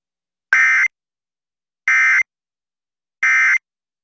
Emergency Alert System (SAME) tones, heard as harsh buzzing bursts. It is heard from 0.5 to 3.6 s.